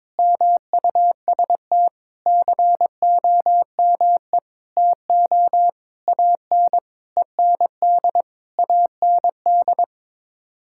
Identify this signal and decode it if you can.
Morse code, heard as a keyed tone, about 22 wpm, 700 Hz. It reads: MUST COME TO AN END AND
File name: Sets of 9 Words - 22wpm